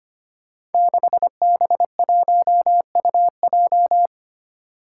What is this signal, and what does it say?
Morse code, heard as a keyed tone, about 25 wpm, 700 Hz. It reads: 6B1UJ
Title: Five Challenging Letter-Number Combinations 25wpm